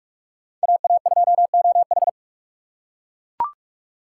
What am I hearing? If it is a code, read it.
Morse code, heard as a keyed tone, about 40 wpm, 700 Hz. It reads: AA2OH